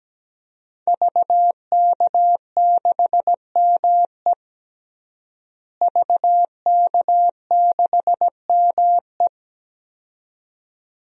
Morse code, heard as a keyed tone, about 17 wpm, 700 Hz. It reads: VK6ME VK6ME
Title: DXCC-100 Call Signs - 5 Characters - Repeated 2x - 17wpm